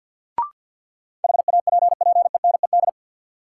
Morse code, heard as a keyed tone, about 52 wpm, 700 Hz. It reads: HAPPENED
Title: Sentences from Top 1000 Words - Review Part 2 - New Word Drill - 50wpm